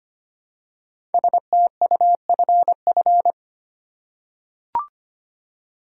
Morse code, heard as a keyed tone, about 25 wpm, 700 Hz. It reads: STUFF
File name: 5 Letter Words - First 3 Letters Given - 25wpm